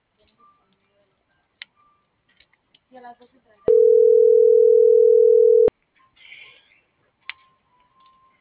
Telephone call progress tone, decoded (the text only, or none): ringback tone